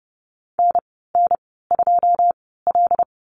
Morse code, heard as a keyed tone, about 30 wpm, spaced as for 17 wpm, 700 Hz. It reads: NN2L
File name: Lesson 14g - Number 2 - Call Signs 30wpm